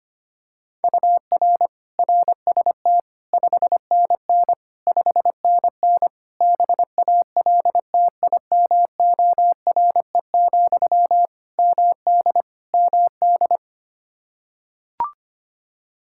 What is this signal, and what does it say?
Morse code, heard as a keyed tone, about 25 wpm, 700 Hz. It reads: UR RST 5NN 5NN BALTIMORE, MD MD
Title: Practice QSO - Exchange 1 - Beginning - 25wpm